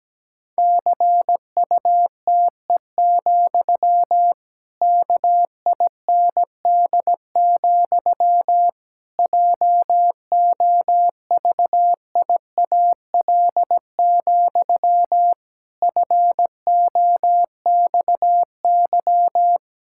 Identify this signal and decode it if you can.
Morse code, heard as a keyed tone, about 17 wpm, 700 Hz. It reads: CUTE, KIND, JOVIAL, FOXY
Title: Pangram Challenge v2 - 17wpm